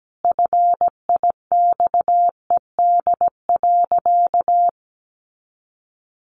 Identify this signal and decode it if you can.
Morse code, heard as a keyed tone, about 17 wpm, 700 Hz. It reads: FIXED.
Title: Sentences from Top 1100 Words 17wpm